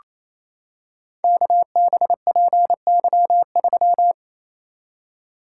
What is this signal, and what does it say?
Morse code, heard as a keyed tone, about 28 wpm, 700 Hz. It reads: KBPY3
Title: Code Groups - 5 Letter:Numbers - 28wpm